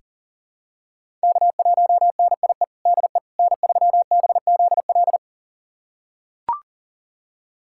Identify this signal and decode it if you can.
Morse code, heard as a keyed tone, about 40 wpm, 700 Hz. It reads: K1NIE DE N3BZL
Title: USA Callsign DE Callsign - 5 Characters - 40wpm